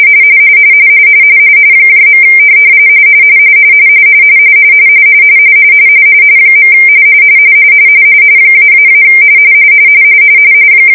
Two-tone teleprinter signal, about 170 Hz shift.